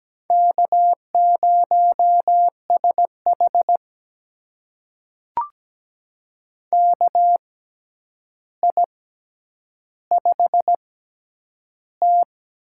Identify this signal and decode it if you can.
Morse code, heard as a keyed tone, about 17 wpm, spaced as for 9 wpm, 700 Hz. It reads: K0SH KI5T
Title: USA Call Signs - 4 Characters - 6x Character Spacing - 17wpm